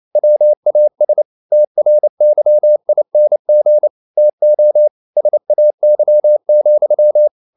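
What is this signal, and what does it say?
Morse code, heard as a keyed tone, about 28 wpm, 600 Hz. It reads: WAS TRYING TO SAY,
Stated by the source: Morse code practice at 28 wpm, text of Ken Follett's novel Never in English, from CW med Gnister.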